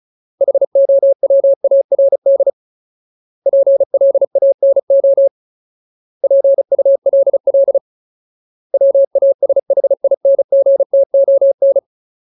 Morse code, heard as a keyed tone, about 34 wpm, 550 Hz. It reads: HOWARD PLANO PULL WASHINGTON
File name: Ultimate Word List / 35WPM Ultimate